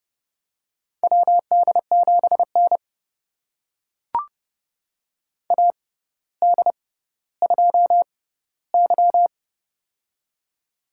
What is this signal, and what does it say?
Morse code, heard as a keyed tone, about 30 wpm, spaced as for 15 wpm, 700 Hz. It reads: WD7N AD2Y
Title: USA Call Signs - 4 Characters - 6x Character Spacing - 30wpm